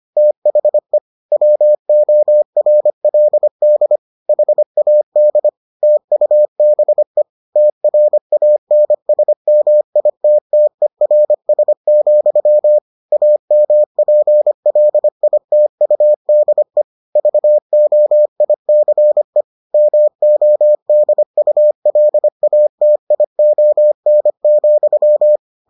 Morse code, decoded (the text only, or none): THE WORLD HAD TUBE TRANSMITTERS, AMPLITUDE VOICE MODULATION,